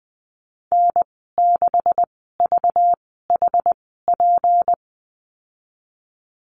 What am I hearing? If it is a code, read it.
Morse code, heard as a keyed tone, about 20 wpm, spaced as for 14 wpm, 700 Hz. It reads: N6VHP
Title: 193 Lesson 31g - Letter v - Call Signs 20wpm